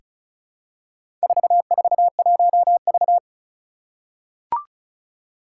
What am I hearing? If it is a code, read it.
Morse code, heard as a keyed tone, about 36 wpm, 700 Hz. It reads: 441V
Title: Four Challenging Letter-Number Combinations 35wpm